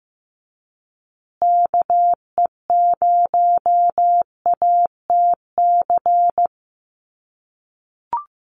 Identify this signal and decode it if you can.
Morse code, heard as a keyed tone, about 15 wpm, 700 Hz. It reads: KE0ATC
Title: USA Call Signs Repeated 3x 15wpm